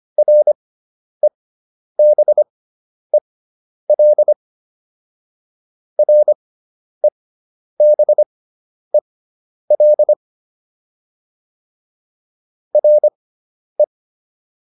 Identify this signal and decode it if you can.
Morse code, heard as a keyed tone, about 25 wpm, spaced as for 10 wpm, 600 Hz. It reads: REBEL REBEL RE